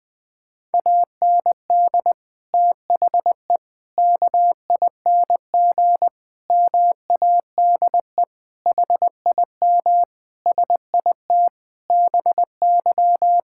Morse code, heard as a keyed tone, about 20 wpm, 700 Hz. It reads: AND THE KING MADE HIM SIT BY
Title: Sentences from Top 1600 Words 20wpm